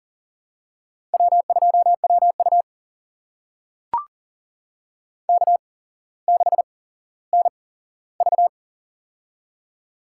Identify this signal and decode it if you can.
Morse code, heard as a keyed tone, about 40 wpm, spaced as for 17 wpm, 700 Hz. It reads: W2WU K6NV
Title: USA Call Signs - 4 Characters - 8x Character Spacing - 40wpm